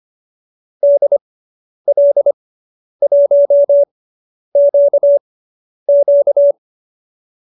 Morse code, heard as a keyed tone, about 25 wpm, spaced as for 10 wpm, 590 Hz. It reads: DL1QQ